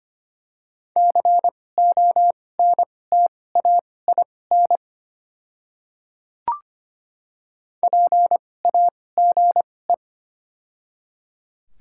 Morse code, heard as a keyed tone, about 25 wpm, spaced as for 18 wpm, 700 Hz. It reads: CONTAIN PAGE
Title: 2x Character Spacing - Top 500 Words - 25wpm